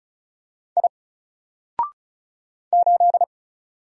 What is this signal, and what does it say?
Morse code, heard as a keyed tone, about 35 wpm, 700 Hz. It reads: I 8
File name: Single-Letter-Number-35wpm